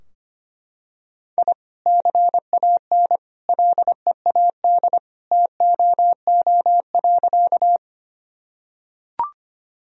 Morse code, heard as a keyed tone, about 25 wpm, 700 Hz. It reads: I CAN LEAD TOO.
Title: Sentences from Top 1000 Words - Review Part 2 - 25wpm